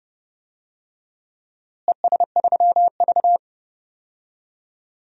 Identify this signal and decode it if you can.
Morse code, heard as a keyed tone, about 30 wpm, 700 Hz. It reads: ES3V
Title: DXCC-100 Call Signs - 4 Characters - Repeated 2x - 30wpm